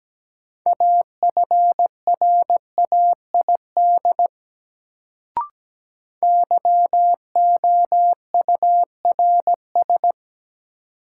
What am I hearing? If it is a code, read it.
Morse code, heard as a keyed tone, about 17 wpm, 700 Hz. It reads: AFRAID YOURS
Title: Sentences from Top 1500 Words - New Word Drill Alt - 17wpm